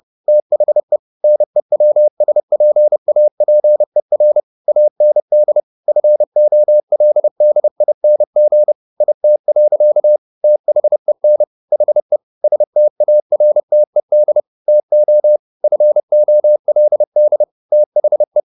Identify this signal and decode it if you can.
Morse code, heard as a keyed tone, about 30 wpm, 600 Hz. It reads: THE NEWSPAPER AND FOLDING IT. THEN HE STARTED TO FOLD THE
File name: the old man and the sea / the old man and the sea 1 30wpm